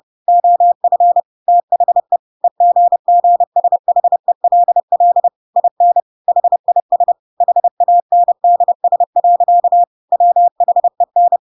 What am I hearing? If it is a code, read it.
Morse code, heard as a keyed tone, about 30 wpm, 700 Hz. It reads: OF THE EGGSHELL IN HIS HANDS. WHEN
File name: Pinocchio / Pinocchio ch5 30 30 ik1hge